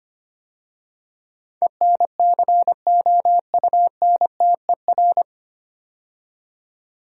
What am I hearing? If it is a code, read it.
Morse code, heard as a keyed tone, about 25 wpm, 700 Hz. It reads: ENCOUNTER
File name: practice / 25wpm / 9 Letter Words - First 3 Letters Given - 25wpm